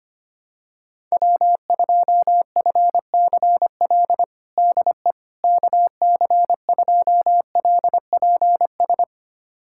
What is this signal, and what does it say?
Morse code, heard as a keyed tone, about 25 wpm, 700 Hz. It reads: W2FCL DE KC2LPS
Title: USA Callsign DE Callsign 25wpm